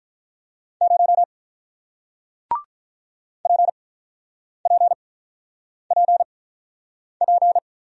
Morse code, heard as a keyed tone, about 48 wpm, spaced as for 9 wpm, 700 Hz. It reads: 0 PPPP